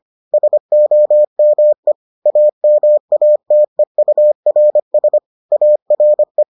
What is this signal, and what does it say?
Morse code, heard as a keyed tone, about 25 wpm, 600 Hz. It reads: SOME AMATEURS ARE